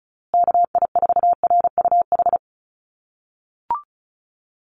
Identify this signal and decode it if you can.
Morse code, heard as a keyed tone, about 34 wpm, 700 Hz. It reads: KI4RUH
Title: USA Call Signs Repeated 2x 35wpm